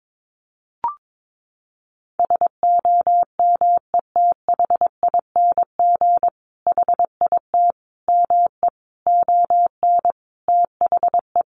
Morse code, heard as a keyed tone, about 22 wpm, 700 Hz. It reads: SOMETHING HIT ME ON THE
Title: Sentences from Top 1200 Words 22wpm